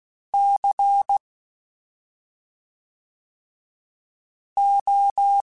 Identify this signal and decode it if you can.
Morse code, heard as a keyed tone, about 16 wpm, 800 Hz. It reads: C O
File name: Special INGs / Special INGs-mp3s / SlowToFaster ing @5